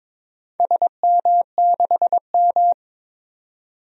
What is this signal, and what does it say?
Morse code, heard as a keyed tone, about 22 wpm, 700 Hz. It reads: SM6M
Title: DXCC-100 Call Signs - 4 Characters - 22wpm